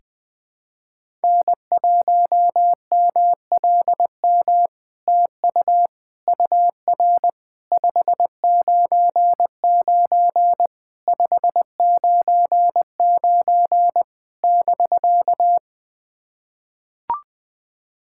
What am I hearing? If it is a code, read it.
Morse code, heard as a keyed tone, about 20 wpm, 700 Hz. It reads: N1MLM TU UR 599 599 <BK>
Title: SOTA QSO Practice 20wpm